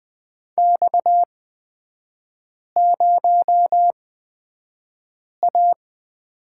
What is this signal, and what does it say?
Morse code, heard as a keyed tone, about 20 wpm, spaced as for 5 wpm, 690 Hz. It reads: X0A
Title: 236 Lesson 38e - Letter q - Three Characters 20wpm